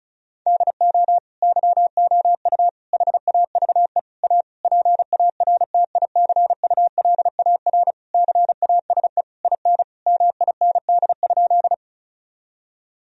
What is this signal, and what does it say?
Morse code, heard as a keyed tone, about 36 wpm, 700 Hz. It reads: DO YOU HAVE A PARTICULAR CASE IN MIND?
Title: Sentences from Top 1000 Words - Review Part 4 - 35wpm